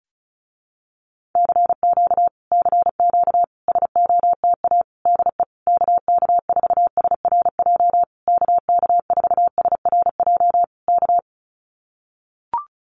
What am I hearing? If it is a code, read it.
Morse code, heard as a keyed tone, about 36 wpm, 700 Hz. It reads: CQ CQ SOTA DE KK4SRJ KK4SRJ K
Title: Common QSO Phrases Encore 35wpm